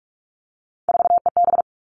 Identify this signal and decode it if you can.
Morse code, heard as a keyed tone, about 44 wpm, 700 Hz. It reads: 4EB
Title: All Cut Numbers - 3-Digit 45wpm